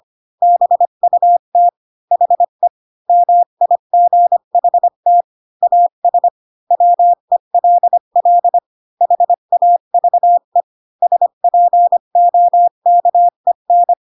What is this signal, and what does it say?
Morse code, heard as a keyed tone, about 25 wpm, 700 Hz. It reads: BUT HE MIGHT AS WELL HAVE SPOKEN
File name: Pinocchio / Pinocchio ch3 20 25 ik1hge